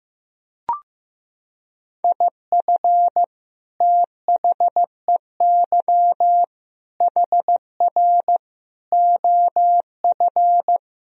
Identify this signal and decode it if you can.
Morse code, heard as a keyed tone, about 15 wpm, 700 Hz. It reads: IF THEY HR OF
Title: Sentences from Top 400 Words and Abbreviations 15wpm